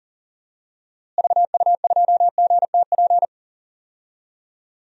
Morse code, heard as a keyed tone, about 40 wpm, 700 Hz. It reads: VU2GTP
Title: DXCC-100 Call Signs - 6 Characters - Repeated 3x - 40wpm